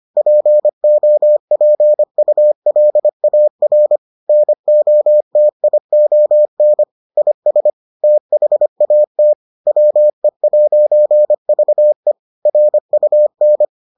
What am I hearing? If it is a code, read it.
Morse code, heard as a keyed tone, about 25 wpm, 600 Hz. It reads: POPULAR NOTION IS THAT WE'VE RUN